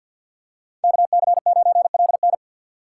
Morse code, heard as a keyed tone, about 53 wpm, 690 Hz. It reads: KC9LN
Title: USA Call Signs - 5 Characters - 3x Character Spacing - 50wpm